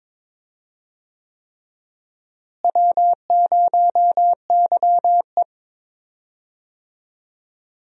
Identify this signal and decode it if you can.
Morse code, heard as a keyed tone, about 21 wpm, 700 Hz. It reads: W0YE
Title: USA Call Signs - 4 Characters - Repeated 3x - 22wpm